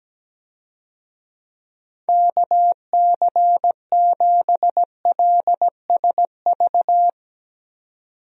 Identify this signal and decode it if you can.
Morse code, heard as a keyed tone, about 17 wpm, 700 Hz. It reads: KC7LSV
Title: USA Call Signs - 6 Characters - 17wpm